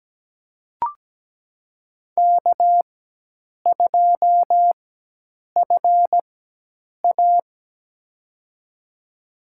Morse code, heard as a keyed tone, about 17 wpm, spaced as for 8 wpm, 700 Hz. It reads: K2FA
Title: USA Call Signs - 4 Characters - 4x Character Spacing - 17wpm